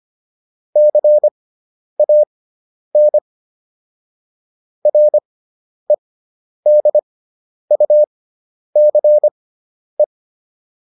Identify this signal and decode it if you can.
Morse code, heard as a keyed tone, about 25 wpm, spaced as for 10 wpm, 600 Hz. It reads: CAN REDUCE